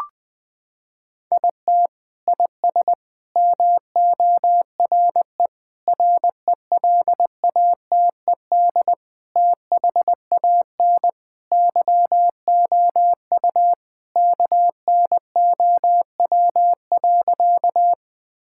Morse code, heard as a keyed tone, about 20 wpm, 700 Hz. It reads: IT IS MORE RELATED THAN YOU KNOW.
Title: Sentences from Top 1000 Words - Review Part 5 - 20wpm